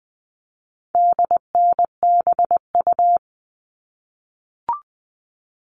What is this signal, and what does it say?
Morse code, heard as a keyed tone, about 20 wpm, 700 Hz. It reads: DNBU
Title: All Cut Numbers - 4-Digit 20wpm